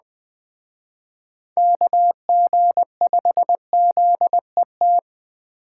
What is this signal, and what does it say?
Morse code, heard as a keyed tone, about 20 wpm, 700 Hz. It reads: KG5ZET
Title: Speed Racing - USA 2x3 Call Signs 17wpm